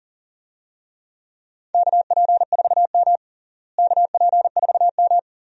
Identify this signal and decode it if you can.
Morse code, heard as a keyed tone, about 40 wpm, 700 Hz. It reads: KP4M KP4M
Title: DXCC-100 Call Signs - 4 Characters - Repeated 2x - 40wpm